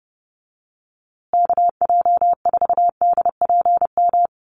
Morse code, heard as a keyed tone, about 30 wpm, 700 Hz. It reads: KJ4DPM
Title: USA Call Signs Repeated 2x 30wpm